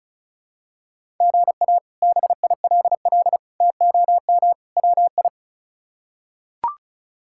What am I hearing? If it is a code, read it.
Morse code, heard as a keyed tone, about 36 wpm, 700 Hz. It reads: GA BILL TOM WI
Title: K1USN SST QSO Practice - 35wpm